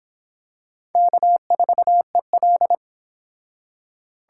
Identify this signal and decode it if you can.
Morse code, heard as a keyed tone, about 25 wpm, 700 Hz. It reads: K4EL